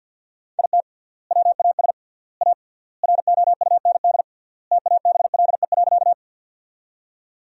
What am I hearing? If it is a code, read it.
Morse code, heard as a keyed tone, about 50 wpm, 700 Hz. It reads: IT WAS A ROUND TABLE.
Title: 3x Word-Spacing - Sentences from Top 400 Words - 50wpm